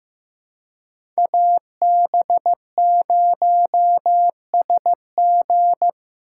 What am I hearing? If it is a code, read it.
Morse code, heard as a keyed tone, about 16 wpm, 700 Hz. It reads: AB0SG
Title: USA Call Signs - 5 Characters - 2x Character Spacing - 15wpm